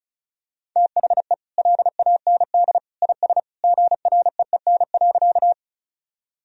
Morse code, heard as a keyed tone, about 36 wpm, 700 Hz. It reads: THE LAND IS GREEN.
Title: Sentences from Top 1000 Words - Review Part 2 - 35wpm